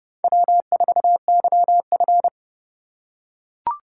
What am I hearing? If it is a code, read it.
Morse code, heard as a keyed tone, about 29 wpm, 700 Hz. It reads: W4YF